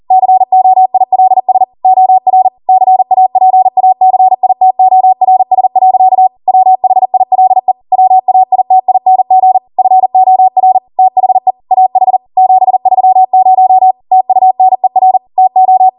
Morse code, heard as a keyed tone, about 40 wpm, 750 Hz. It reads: COILS OR CAPACITORS. WHILE WAITING FOR THE AH 730 TUNER TO